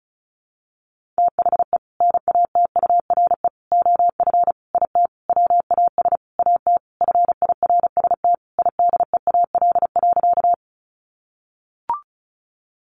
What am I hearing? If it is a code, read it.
Morse code, heard as a keyed tone, about 34 wpm, 700 Hz. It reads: THE NATURE OF IT WAS AT FIRST IDEAL.
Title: Sentences from Top 1400 Words 35wpm